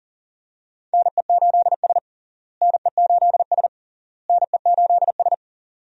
Morse code, heard as a keyed tone, about 40 wpm, 700 Hz. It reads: NE8S NE8S NE8S
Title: USA Call Signs - 4 Characters - Repeated 3x - 40wpm